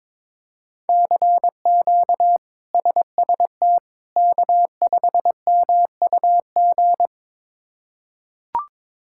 Morse code, heard as a keyed tone, about 22 wpm, 700 Hz. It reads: CQ SST K5MUG
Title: K1USN SST QSO Practice - 22wpm